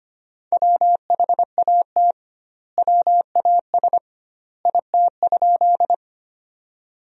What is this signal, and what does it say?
Morse code, heard as a keyed tone, about 25 wpm, 700 Hz. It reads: WHAT WAS IT?
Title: 2x Word-Spacing - Sentences from Top 100 Words - 25wpm